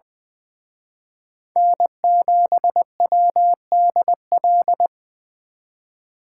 Morse code, heard as a keyed tone, about 24 wpm, 700 Hz. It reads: N7WDL
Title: Speed Racing - USA 1x3 Call Signs 20wpm